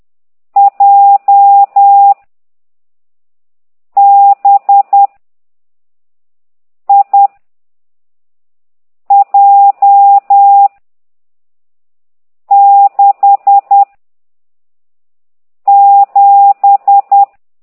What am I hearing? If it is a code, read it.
Morse code, heard as a keyed tone, about 10 wpm, 800 Hz. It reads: J B I J 6 7
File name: ct2idw / exerc15